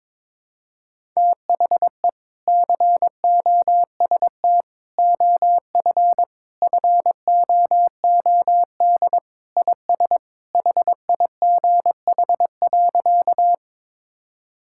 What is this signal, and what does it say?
Morse code, heard as a keyed tone, about 22 wpm, 700 Hz. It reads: THE COST OF FOOD IS HIGH.
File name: practice / 22wpm / Sentences with 3-Word Phrases - 22wpm